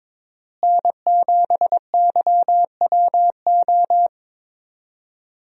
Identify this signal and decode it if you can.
Morse code, heard as a keyed tone, about 22 wpm, 700 Hz. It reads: N7YWO
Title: USA Call Signs - 5 Characters - Repeated 3x - 22wpm